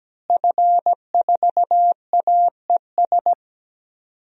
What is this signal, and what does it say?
Morse code, heard as a keyed tone, about 17 wpm, 700 Hz. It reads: F4AES
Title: DXCC-100 Call Signs - 5 Characters - Repeated 2x - 17wpm